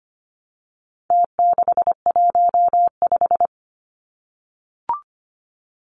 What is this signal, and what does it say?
Morse code, heard as a keyed tone, about 25 wpm, 700 Hz. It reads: T615
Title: Numbers - Contest Style - 4-Digit 25wpm